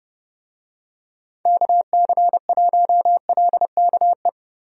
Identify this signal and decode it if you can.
Morse code, heard as a keyed tone, about 29 wpm, 700 Hz. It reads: KC1LKE
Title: USA Call Signs - 6 Characters - Repeated 3x - 30wpm